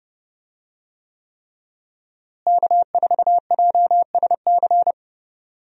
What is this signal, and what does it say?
Morse code, heard as a keyed tone, about 30 wpm, 700 Hz. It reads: K4JSC
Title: USA Call Signs - 5 Characters - Repeated 3x - 30wpm